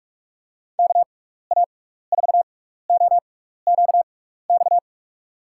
Morse code, heard as a keyed tone, about 43 wpm, spaced as for 15 wpm, 700 Hz. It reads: KA4OQX
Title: USA Call Signs - 6 Characters - 6x Character Spacing - 45wpm